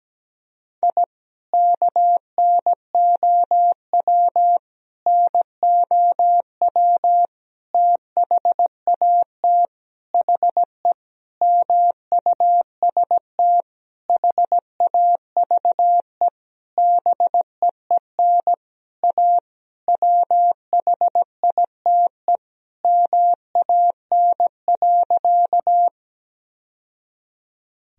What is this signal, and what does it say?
Morse code, heard as a keyed tone, about 17 wpm, 700 Hz. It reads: I KNOW NOW THAT HE MUST HAVE BEEN A WHITE MAN.
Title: Sentences from Top 1000 Words - Review Part 1 - 17wpm